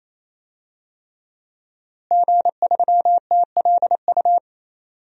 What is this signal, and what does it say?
Morse code, heard as a keyed tone, about 28 wpm, 700 Hz. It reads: G3TLU
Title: DXCC-100 Call Signs - 5 Characters - Repeated 3x - 28wpm